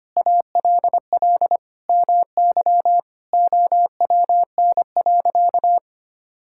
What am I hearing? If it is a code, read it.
Morse code, heard as a keyed tone, about 25 wpm, 700 Hz. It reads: ALL MY OWN.
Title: Sentences from Top 1000 Words - Review Part 6 - 25wpm